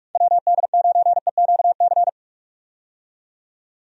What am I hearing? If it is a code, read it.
Morse code, heard as a keyed tone, about 46 wpm, 700 Hz. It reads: WD9EQC
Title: USA Call Signs - 6 Characters - 45wpm